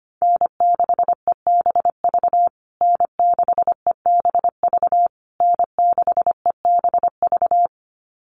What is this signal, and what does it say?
Morse code, heard as a keyed tone, about 25 wpm, 700 Hz. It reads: N6EBV N6EBV N6EBV
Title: USA Call Signs Repeated 3x 25wpm